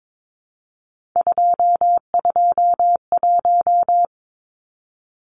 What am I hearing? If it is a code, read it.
Morse code, heard as a keyed tone, about 22 wpm, 700 Hz. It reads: 221
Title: Numbers - 3-Digit 22wpm